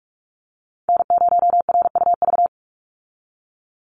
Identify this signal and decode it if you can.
Morse code, heard as a keyed tone, about 46 wpm, 700 Hz. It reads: N0RUV
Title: USA Call Signs 45wpm